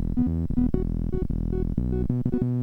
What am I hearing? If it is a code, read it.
There is no code.